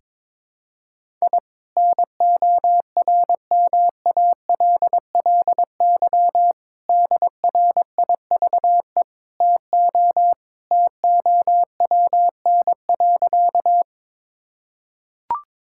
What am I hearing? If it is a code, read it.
Morse code, heard as a keyed tone, about 22 wpm, 700 Hz. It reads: I NORMALLY DRIVE TO TOWN.
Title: Sentences from Top 1000 Words - Review Part 6 - 22wpm